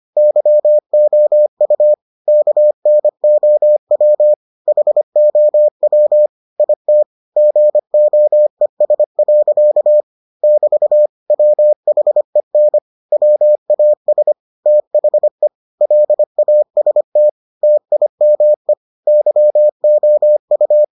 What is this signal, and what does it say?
Morse code, heard as a keyed tone, about 25 wpm, 600 Hz. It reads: YOU KNOW HOW IT GOES. <BT> WHEN WAS THE LAST TIME YOU